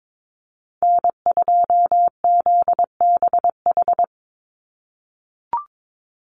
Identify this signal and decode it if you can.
Morse code, heard as a keyed tone, about 22 wpm, 700 Hz. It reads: N2ZBH
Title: USA Call Signs 22wpm